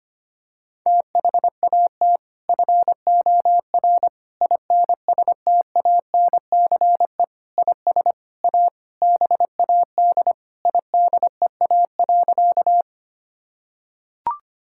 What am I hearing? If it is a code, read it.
Morse code, heard as a keyed tone, about 25 wpm, 700 Hz. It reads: THAT FOR INSTANCE IS A BAD IDEA.
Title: Sentences from Top 1000 Words - Review Part 5 - 25wpm